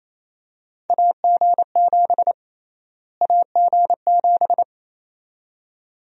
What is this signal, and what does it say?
Morse code, heard as a keyed tone, about 28 wpm, 700 Hz. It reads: AG7 AG7
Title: USA Call Sign Prefixes - Repeated 2x - 28wpm